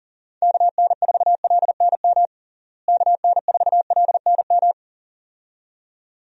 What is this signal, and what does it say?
Morse code, heard as a keyed tone, about 37 wpm, 700 Hz. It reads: KN4LNM KN4LNM
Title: USA Call Signs - 6 Characters - Repeated 2x - 40wpm